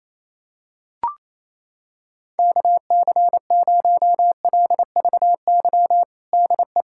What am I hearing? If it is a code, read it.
Morse code, heard as a keyed tone, about 28 wpm, 700 Hz. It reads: KC0LVY DE
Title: USA Callsign DE Callsign 28wpm